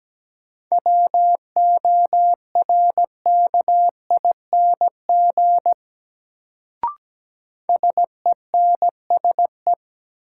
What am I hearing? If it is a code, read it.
Morse code, heard as a keyed tone, about 17 wpm, 700 Hz. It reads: WORKING SENSE
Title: Sentences from Top 700 Words - New Word Drill - 17wpm